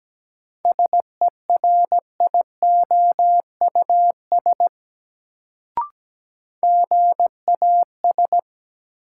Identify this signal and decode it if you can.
Morse code, heard as a keyed tone, about 17 wpm, 700 Hz. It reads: SERIOUS GAS
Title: Sentences from Top 1100 Words - New Word Drill Alt - 17wpm